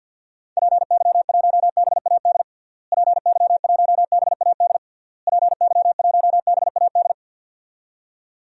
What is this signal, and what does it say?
Morse code, heard as a keyed tone, about 49 wpm, 700 Hz. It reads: PY1BAD PY1BAD PY1BAD
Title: DXCC-100 Call Signs - 6 Characters - Repeated 3x - 50wpm